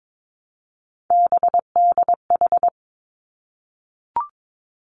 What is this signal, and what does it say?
Morse code, heard as a keyed tone, about 22 wpm, 700 Hz. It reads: BDH
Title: Three Challenging Letter-Number Combinations 22wpm